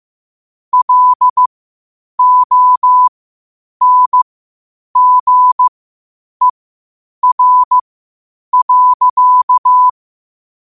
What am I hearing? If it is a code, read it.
Morse code, heard as a keyed tone, about 15 wpm, spaced as for 8 wpm, 1000 Hz. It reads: LONGER.
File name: qotdfiles / 8wpm